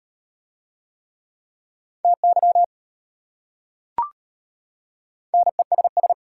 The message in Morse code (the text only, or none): TY NESS